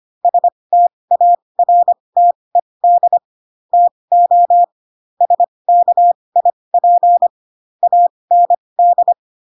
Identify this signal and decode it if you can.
Morse code, heard as a keyed tone, about 25 wpm, spaced as for 20 wpm, 700 Hz. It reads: STARTED TO SKIP AND
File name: Pinocchio / Pinocchio ch8 15 25 ik1hge